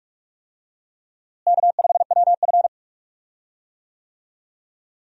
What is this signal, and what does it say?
Morse code, heard as a keyed tone, about 46 wpm, 700 Hz. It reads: K5WF
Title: USA Call Signs - 4 Characters - Repeated 3x - 45wpm